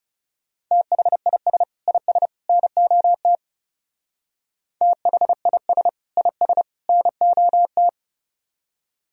Morse code, heard as a keyed tone, about 37 wpm, 700 Hz. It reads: THIS IS NOT THIS IS NOT
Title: Speed-Racing - Sets of 3 Words 30wpm